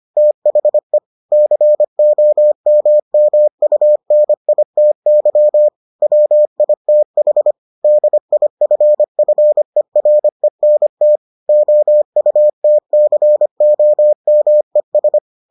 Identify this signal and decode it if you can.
Morse code, heard as a keyed tone, about 25 wpm, 600 Hz. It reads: THE COMMUNITY WITH DIFFERENT OUTCOMES